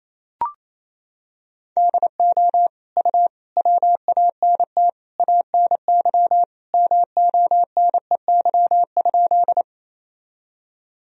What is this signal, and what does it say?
Morse code, heard as a keyed tone, about 28 wpm, 700 Hz. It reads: DO U WANT ANY MONEY?
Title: Sentences from Top 400 Words and Abbreviations 28wpm